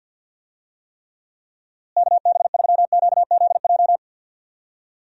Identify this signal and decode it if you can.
Morse code, heard as a keyed tone, about 51 wpm, 700 Hz. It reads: KB3QZJ
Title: USA Call Signs - 6 Characters - Repeated 3x - 50wpm